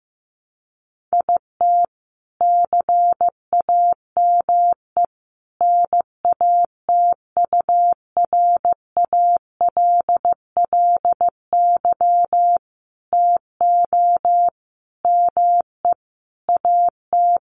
Morse code, heard as a keyed tone, about 15 wpm, 700 Hz. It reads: IT CAME NATURALLY TO ME AT
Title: Sentences from Top 1300 Words 15wpm